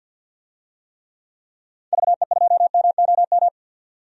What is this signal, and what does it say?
Morse code, heard as a keyed tone, about 46 wpm, 700 Hz. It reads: VE2MOM
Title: DXCC-100 Call Signs - 6 Characters - 50wpm